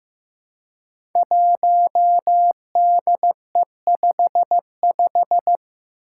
Morse code, heard as a keyed tone, about 15 wpm, 700 Hz. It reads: 1DE55
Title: Five Challenging Letter-Number Combinations 15wpm